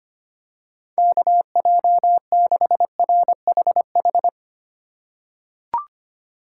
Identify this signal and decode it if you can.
Morse code, heard as a keyed tone, about 25 wpm, 700 Hz. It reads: KJ6RHH
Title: USA Call Signs - 6 Characters - 4x Character Spacing - 25wpm